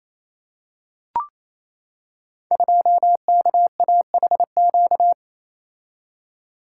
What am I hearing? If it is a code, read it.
Morse code, heard as a keyed tone, about 29 wpm, 700 Hz. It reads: 2KAHQ
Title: Code Groups - 5 Letter:Numbers - 28wpm